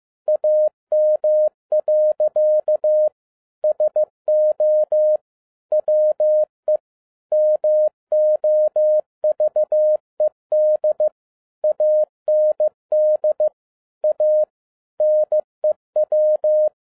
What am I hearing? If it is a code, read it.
Morse code, heard as a keyed tone, about 15 wpm, 610 Hz. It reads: AM. SO WE MOVED AND A NEW
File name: 2011-07-mp3 / n07 A LOT HAS CHANGED 15wpm